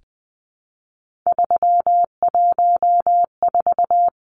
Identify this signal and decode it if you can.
Morse code, heard as a keyed tone, about 20 wpm, 700 Hz. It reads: 314
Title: Numbers - 3-Digit 20wpm